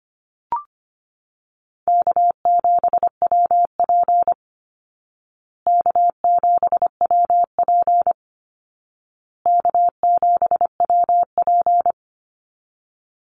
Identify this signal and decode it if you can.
Morse code, heard as a keyed tone, about 25 wpm, 700 Hz. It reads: K7WP K7WP K7WP
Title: Easy Call Signs Repeated 3x - 25wpm